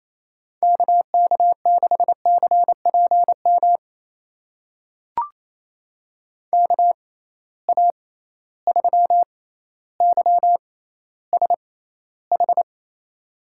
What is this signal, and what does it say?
Morse code, heard as a keyed tone, about 28 wpm, spaced as for 14 wpm, 700 Hz. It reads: KK6CPM KA3YSH
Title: USA Call Signs - 6 Characters - 6x Character Spacing - 28wpm